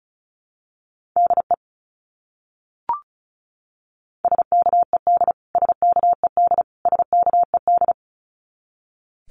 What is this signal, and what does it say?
Morse code, heard as a keyed tone, about 34 wpm, 700 Hz. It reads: DE SKED SKED SKED
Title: Abbreviations Repeated 3x - 35wpm